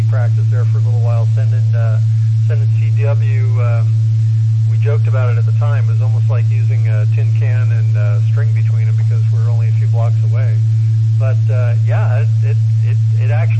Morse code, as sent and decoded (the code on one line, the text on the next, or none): none